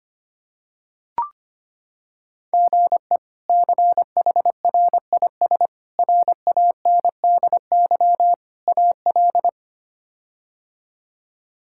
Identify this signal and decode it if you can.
Morse code, heard as a keyed tone, about 25 wpm, 700 Hz. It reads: GE CHRIS RANDY AL
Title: K1USN SST QSO Practice - 25wpm